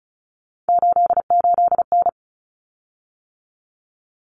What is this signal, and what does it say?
Morse code, heard as a keyed tone, about 34 wpm, 700 Hz. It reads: 88N
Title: Common Cut Numbers - 3-Digit 35wpm